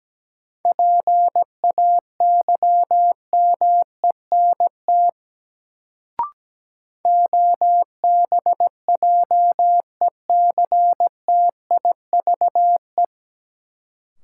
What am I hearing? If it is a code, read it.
Morse code, heard as a keyed tone, about 17 wpm, 700 Hz. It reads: PAYMENT OBJECTIVE
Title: Sentences from Top 1200 Words - New Word Drill - 17wpm